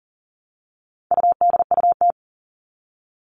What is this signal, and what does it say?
Morse code, heard as a keyed tone, about 40 wpm, 700 Hz. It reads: UDUT
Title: All Cut Numbers - 4-Digit 40wpm